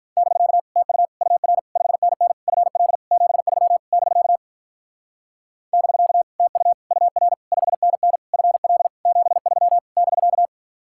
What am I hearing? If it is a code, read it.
Morse code, heard as a keyed tone, about 46 wpm, 700 Hz. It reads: <BK> TU UR 5NN FL 73 <BK> <BK> TU UR 5NN FL 73 <BK>